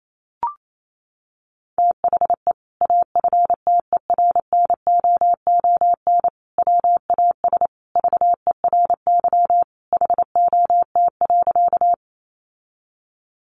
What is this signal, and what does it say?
Morse code, heard as a keyed tone, about 28 wpm, 700 Hz. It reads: THE AFTERNOON WAS VERY HOT.
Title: Sentences from Top 1500 Words 28wpm